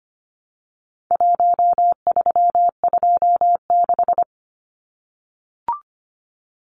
Morse code, heard as a keyed tone, about 25 wpm, 700 Hz. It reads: 1326
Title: Numbers - 4-Digit 25wpm